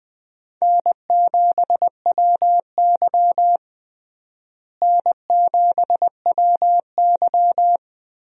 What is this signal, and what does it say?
Morse code, heard as a keyed tone, about 20 wpm, 700 Hz. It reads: N7WY N7WY
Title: USA Call Signs - 4 Characters - Repeated 3x - 20wpm